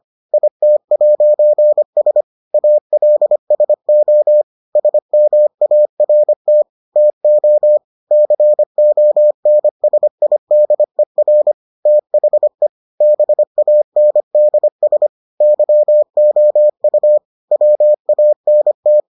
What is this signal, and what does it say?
Morse code, heard as a keyed tone, about 25 wpm, 600 Hz. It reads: IT'S ALSO SMART TO CONSIDER THE BANDS YOU WANT